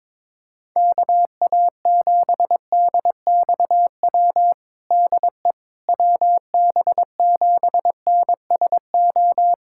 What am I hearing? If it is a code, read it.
Morse code, heard as a keyed tone, about 22 wpm, 700 Hz. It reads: KA7DXW DE WB7NSO
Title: USA Callsign DE Callsign 22wpm